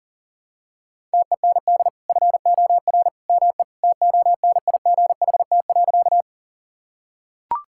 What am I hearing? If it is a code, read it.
Morse code, heard as a keyed tone, about 40 wpm, 700 Hz. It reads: TEND FOR ME TONIGHT.
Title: Sentences from Top 1000 Words - Review Part 5 - 40wpm